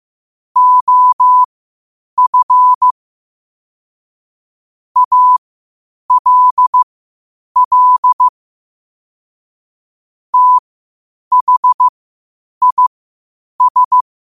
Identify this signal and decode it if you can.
Morse code, heard as a keyed tone, about 15 wpm, spaced as for 8 wpm, 1000 Hz. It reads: OF ALL THIS
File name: qotdfiles / 8wpm